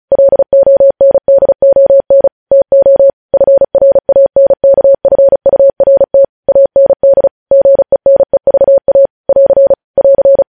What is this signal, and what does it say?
Morse code, heard as a keyed tone, about 36 wpm, 550 Hz. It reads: LONDON TO FRANKFURT AND GENEVA <AR> <AR>
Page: news headlines in morse code at 35 wpm